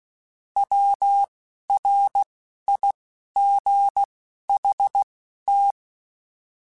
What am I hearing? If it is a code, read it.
Morse code, heard as a keyed tone, about 16 wpm, spaced as for 11 wpm, 800 Hz. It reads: WRIGHT
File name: Misc / Misc-mp3s / Words @10